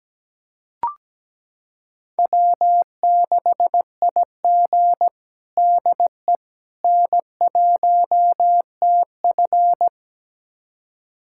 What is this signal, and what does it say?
Morse code, heard as a keyed tone, about 17 wpm, 700 Hz. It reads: W6IG DE N1TF
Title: USA Callsign DE Callsign - 4 Characters - 17wpm